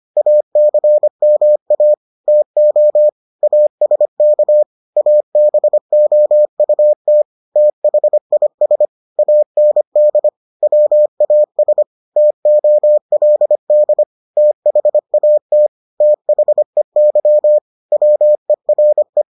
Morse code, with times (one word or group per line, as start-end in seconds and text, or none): 0.2-1.9 ACMA
2.3-3.1 TO
3.4-4.6 ASK
5.0-7.2 ABOUT
7.6-8.9 THIS
9.2-10.3 AND
10.6-11.8 WAS
12.2-14.0 TOLD
14.4-15.7 THAT
16.0-17.6 THEY
17.9-19.2 WERE